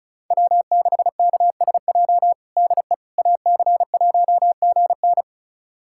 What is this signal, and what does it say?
Morse code, heard as a keyed tone, about 36 wpm, 700 Hz. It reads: W6KSJ DE AC1GN
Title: USA Callsign DE Callsign - 5 Characters - 35wpm